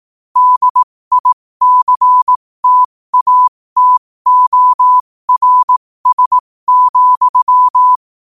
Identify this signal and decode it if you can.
Morse code, heard as a keyed tone, about 18 wpm, spaced as for 15 wpm, 1000 Hz. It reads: DICTATORS,
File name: qotdfiles / 15wpm